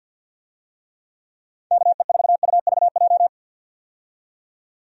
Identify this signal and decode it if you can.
Morse code, heard as a keyed tone, about 51 wpm, 700 Hz. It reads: KE4UVJ
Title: USA Call Signs - 6 Characters - 50wpm